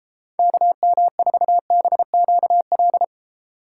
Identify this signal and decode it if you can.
Morse code, heard as a keyed tone, about 29 wpm, 700 Hz. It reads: KM4BQL